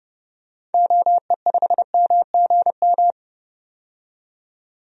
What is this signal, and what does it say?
Morse code, heard as a keyed tone, about 32 wpm, 700 Hz. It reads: OE5MGM
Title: DXCC-100 Call Signs - 6 Characters - 30wpm